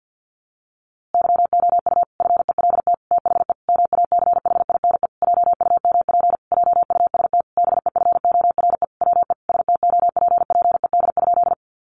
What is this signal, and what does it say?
Morse code, heard as a keyed tone, about 50 wpm, 700 Hz. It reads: YOU FELT THE MACHINE JUMP JUST BEFORE WE STOPPED?
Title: A Sound Of Thunder 50wpm